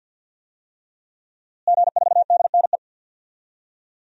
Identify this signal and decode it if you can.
Morse code, heard as a keyed tone, about 50 wpm, 700 Hz. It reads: G4DNE
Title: DXCC-100 Call Signs - 5 Characters - Repeated 3x - 50wpm